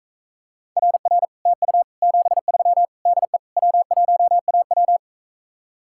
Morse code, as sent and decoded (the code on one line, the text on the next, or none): .-. .-. / - ..- / --... ...-- / -.. / .-- .---- .- .--
RR TU 73 DE W1AW